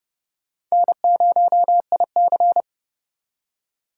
Morse code, heard as a keyed tone, about 37 wpm, 700 Hz. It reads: N0IC
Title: Speed Racing - USA 1x2 Call Signs 30wpm